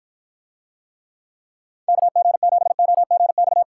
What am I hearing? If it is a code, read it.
Morse code, heard as a keyed tone, about 49 wpm, 700 Hz. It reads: KG7OGX